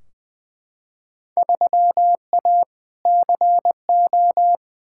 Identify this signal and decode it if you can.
Morse code, heard as a keyed tone, about 20 wpm, 700 Hz. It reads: 3A CO
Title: Field Day Exchanges Encore - 20wpm